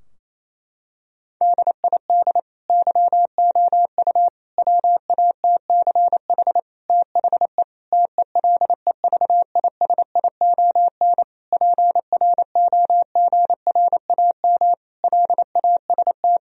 Morse code, heard as a keyed tone, about 28 wpm, 700 Hz. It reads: DID YOU WATCH THE TELEVISION PROGRAM LAST
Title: Sentences from Top 1000 Words - Review Part 4 - 28wpm